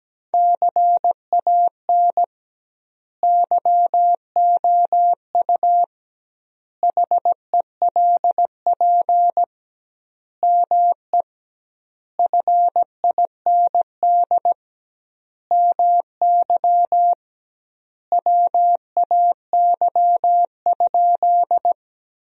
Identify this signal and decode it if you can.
Morse code, heard as a keyed tone, about 17 wpm, 700 Hz. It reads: CAN YOU HELP ME FIND MY WAY?
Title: 2x Word-Spacing - Sentences from Top 200 Words - 17wpm